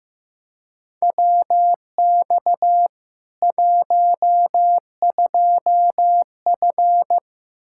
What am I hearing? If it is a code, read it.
Morse code, heard as a keyed tone, about 15 wpm, 700 Hz. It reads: WX 12F
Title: Practice QSO - Exchange 2 - Info - 15wpm